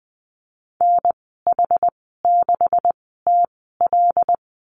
Morse code, heard as a keyed tone, about 20 wpm, spaced as for 15 wpm, 700 Hz. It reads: NH6TL
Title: 128 Lesson 21g - Number 6 - Call Signs 20wpm